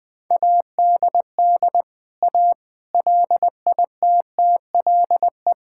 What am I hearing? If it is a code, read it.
Morse code, heard as a keyed tone, about 19 wpm, 700 Hz. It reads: ADD A LITTLE
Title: Sentences with 4-Word Phrases - 20wpm